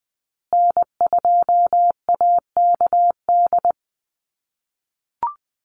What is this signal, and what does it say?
Morse code, heard as a keyed tone, about 20 wpm, 700 Hz. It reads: N2AKD
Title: USA Call Signs Repeated 3x 20wpm